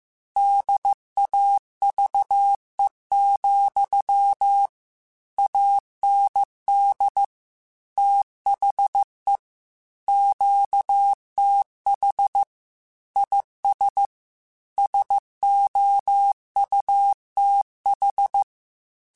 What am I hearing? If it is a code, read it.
Morse code, heard as a keyed tone, about 15 wpm, 800 Hz. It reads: DAVE, AND THE QTH IS SOUTH